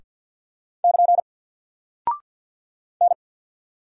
Morse code, heard as a keyed tone, about 48 wpm, 700 Hz. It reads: <KN> N
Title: Single Character (Letter:Number:Punctuation:Prosign) Alt 50wpm